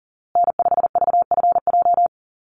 Morse code, heard as a keyed tone, about 40 wpm, 700 Hz. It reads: N5VFJ
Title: USA Call Signs Repeated 3x 40wpm